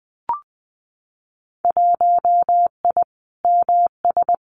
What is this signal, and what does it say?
Morse code, heard as a keyed tone, about 20 wpm, 700 Hz. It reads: 1I MS
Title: Winter Field Day Exchanges Encore 20wpm